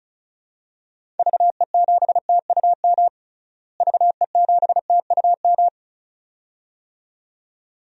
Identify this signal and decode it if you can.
Morse code, heard as a keyed tone, about 36 wpm, 700 Hz. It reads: VE7TUM VE7TUM
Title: DXCC-100 Call Signs - 6 Characters - Repeated 2x - 35wpm